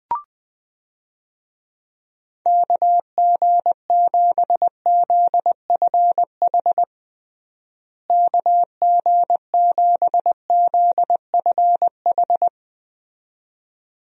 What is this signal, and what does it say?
Morse code, heard as a keyed tone, about 20 wpm, 700 Hz. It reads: KG7ZFH KG7ZFH
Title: USA Call Signs - 6 Characters - Repeated 2x - 20wpm